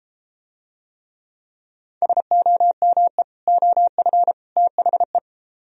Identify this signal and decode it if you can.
Morse code, heard as a keyed tone, about 31 wpm, 700 Hz. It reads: SOME OF THE